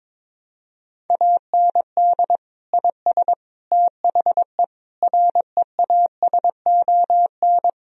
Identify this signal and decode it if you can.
Morse code, heard as a keyed tone, about 22 wpm, 700 Hz. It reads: AND IS THE REASON
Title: Sentences from Top 1000 Words - Review Part 3 - 22wpm